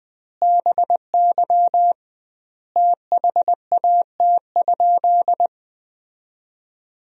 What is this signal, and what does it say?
Morse code, heard as a keyed tone, about 20 wpm, 700 Hz. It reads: BY THAT?
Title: 2x Word-Spacing - Sentences from Top 200 Words - 20wpm